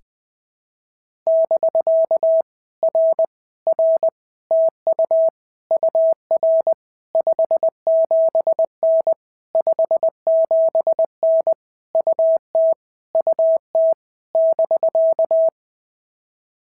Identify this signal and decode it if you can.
Morse code, heard as a keyed tone, about 20 wpm, 650 Hz. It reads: <BK> R R TU UR 57N 57N UT UT <BK>
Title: POTA QSO Practice - 20wpm